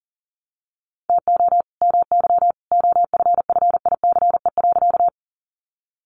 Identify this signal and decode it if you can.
Morse code, heard as a keyed tone, about 40 wpm, 700 Hz. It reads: TO MY OFFICE.
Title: Sentences from Top 1000 Words 40wpm